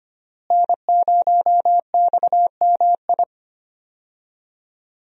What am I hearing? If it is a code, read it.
Morse code, heard as a keyed tone, about 25 wpm, 700 Hz. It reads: N0XMI
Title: USA Call Signs - 5 Characters - Repeated 2x - 25wpm